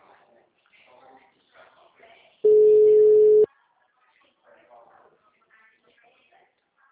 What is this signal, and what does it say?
Ringback tone (Europe, 425 Hz), with speech.
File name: out- -2056- - - .25886